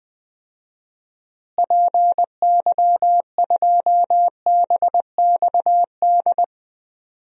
Morse code, heard as a keyed tone, about 21 wpm, 700 Hz. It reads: PY2BXD
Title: DXCC-100 Call Signs - 6 Characters - Repeated 2x - 20wpm